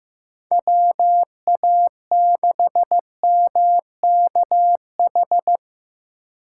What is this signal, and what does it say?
Morse code, heard as a keyed tone, about 15 wpm, 700 Hz. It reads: WA6MKH
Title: USA Call Signs - 4x Character Spacing - 15wpm